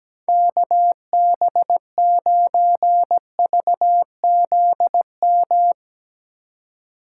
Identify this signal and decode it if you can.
Morse code, heard as a keyed tone, about 19 wpm, 700 Hz. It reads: KB9VZM
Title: Speed Racing - USA 2x3 Call Signs 17wpm